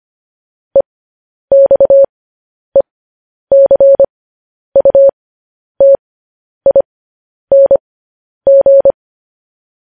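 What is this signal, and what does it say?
Morse code, heard as a keyed tone, about 25 wpm, spaced as for 10 wpm, 550 Hz. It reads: EXECUTING
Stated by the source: Morse code news headlines sent at 10 wpm.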